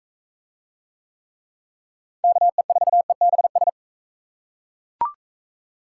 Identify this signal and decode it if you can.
Morse code, heard as a keyed tone, about 40 wpm, 700 Hz. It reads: KE4EBS